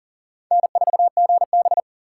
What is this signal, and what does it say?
Morse code, heard as a keyed tone, about 47 wpm, 700 Hz. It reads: N4GB
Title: Speed Racing - USA 1x2 Call Signs 40wpm